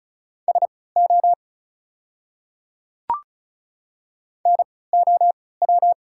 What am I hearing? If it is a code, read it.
Morse code, heard as a keyed tone, about 36 wpm, spaced as for 20 wpm, 690 Hz. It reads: SO NOW
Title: 3x Character Spacing - Top 100 Words - 35wpm